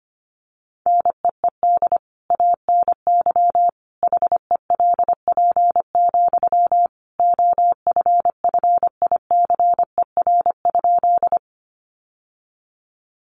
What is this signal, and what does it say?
Morse code, heard as a keyed tone, about 25 wpm, 700 Hz. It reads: NEED ANY HELP, OFFICER?
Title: Sentences from Top 1200 Words 25wpm